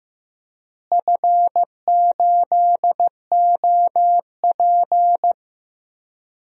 Morse code, heard as a keyed tone, about 14 wpm, 700 Hz. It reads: F8OP
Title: DXCC-100 Call Signs - 4 Characters - Repeated 3x - 15wpm